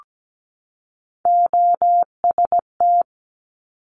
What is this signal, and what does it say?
Morse code, heard as a keyed tone, about 17 wpm, 700 Hz. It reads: OST
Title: Multi-letter Phonemes 17wpm